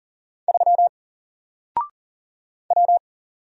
Morse code, heard as a keyed tone, about 43 wpm, 700 Hz. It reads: 3 W
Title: Single Letter-Number Alt 40wpm